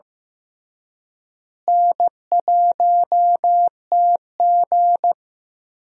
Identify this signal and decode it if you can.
Morse code, heard as a keyed tone, about 15 wpm, 700 Hz. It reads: N1TG
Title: USA Call Signs - 4 Characters - Repeated 3x - 15wpm